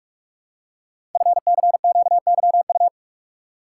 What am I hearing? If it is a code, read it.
Morse code, heard as a keyed tone, about 44 wpm, 700 Hz. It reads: FCQYU
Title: Code Groups - 5 Letters - 45wpm